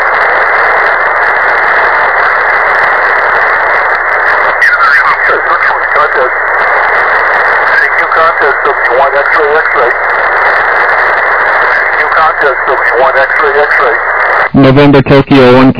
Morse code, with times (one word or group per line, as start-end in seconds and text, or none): none